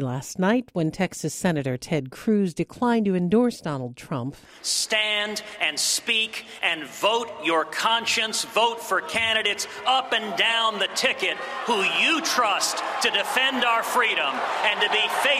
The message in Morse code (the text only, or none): none